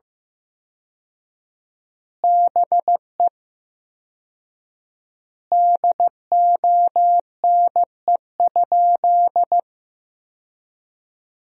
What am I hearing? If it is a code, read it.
Morse code, heard as a keyed tone, about 15 wpm, 700 Hz. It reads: BE DONE?
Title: 4x Word-Spacing - Sentences from Top 500 Words - 15wpm